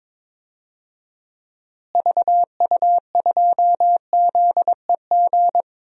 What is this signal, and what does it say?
Morse code, heard as a keyed tone, about 22 wpm, 700 Hz. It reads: VU2ZEG
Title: DXCC-100 Call Signs - 6 Characters - Repeated 3x - 22wpm